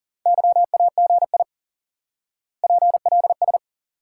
Morse code, heard as a keyed tone, about 37 wpm, 700 Hz. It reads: YAGI PLS
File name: IQE Course 3 - Lesson 89 - YAGI - Two Elements - 40wpm